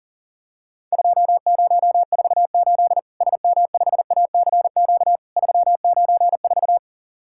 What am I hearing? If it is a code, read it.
Morse code, heard as a keyed tone, about 40 wpm, 700 Hz. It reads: 2048 SM5ACQ 394
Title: Worked All Europe DX Contest - QTC Exchange - 40wpm